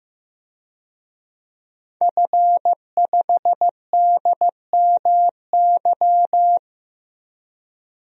Morse code, heard as a keyed tone, about 15 wpm, 700 Hz. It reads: F5DMY
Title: DXCC-100 Call Signs - 5 Characters - Repeated 2x - 15wpm